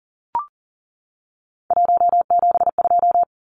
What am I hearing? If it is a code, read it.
Morse code, heard as a keyed tone, about 40 wpm, 700 Hz. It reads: 172
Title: Numbers - 3-Digit 40wpm